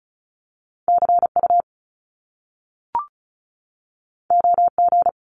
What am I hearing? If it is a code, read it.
Morse code, heard as a keyed tone, about 36 wpm, 700 Hz. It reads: CU OG
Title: Top Two Letter Combinations in Words 35wpm